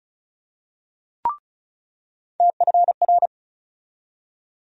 Morse code, heard as a keyed tone, about 36 wpm, 700 Hz. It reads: TFR
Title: USA Call Sign Postfixes - 3 Letters 35wpm